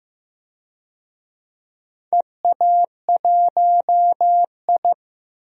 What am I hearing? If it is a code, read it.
Morse code, heard as a keyed tone, about 15 wpm, 700 Hz. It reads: EA1I
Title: DXCC-100 Call Signs - 4 Characters - Repeated 3x - 15wpm